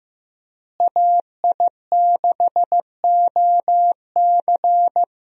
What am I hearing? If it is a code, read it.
Morse code, heard as a keyed tone, about 16 wpm, 700 Hz. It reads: AI6OC
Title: USA Call Signs - 5 Characters - Repeated 2x - 15wpm